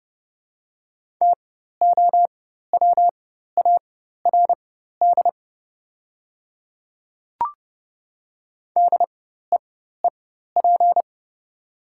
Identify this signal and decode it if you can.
Morse code, heard as a keyed tone, about 30 wpm, spaced as for 14 wpm, 690 Hz. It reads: TOWARD DEEP
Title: 4x Character Spacing - Top 500 Words - 30wpm